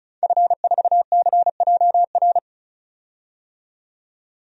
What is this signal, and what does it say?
Morse code, heard as a keyed tone, about 36 wpm, 700 Hz. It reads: F4CJR
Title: DXCC-100 Call Signs - 5 Characters - 35wpm